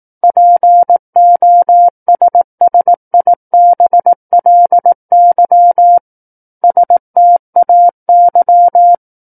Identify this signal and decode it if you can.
Morse code, heard as a keyed tone, about 18 wpm, 700 Hz. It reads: POSSIBLY STAY
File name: 18 WPM